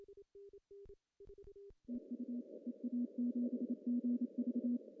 Morse code, heard as a keyed tone, about 28 wpm, 240 Hz. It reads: TU EA7GV
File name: CW / cqww-cw-k5tr